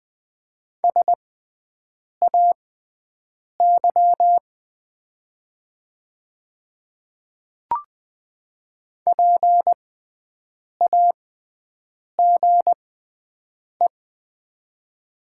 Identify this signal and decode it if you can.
Morse code, heard as a keyed tone, about 20 wpm, spaced as for 7 wpm, 700 Hz. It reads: SAY PAGE
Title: 6x Character Spacing - Top 500 Words - 20wpm